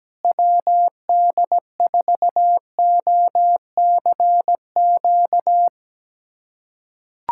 Morse code, heard as a keyed tone, about 17 wpm, 700 Hz. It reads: WD4OCQ
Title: USA Callsign DE Callsign 17wpm